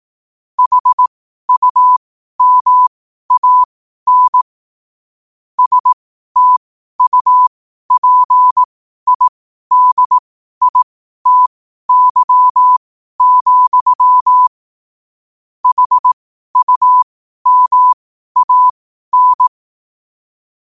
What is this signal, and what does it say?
Morse code, heard as a keyed tone, about 18 wpm, spaced as for 12 wpm, 1000 Hz. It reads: HUMAN STUPIDITY, HUMAN
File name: qotdfiles / 12wpm